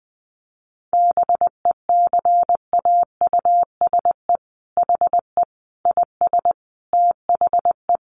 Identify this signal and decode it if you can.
Morse code, heard as a keyed tone, about 20 wpm, 700 Hz. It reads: BECAUSE HE IS THE
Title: Sentences from Top 1500 Words 20wpm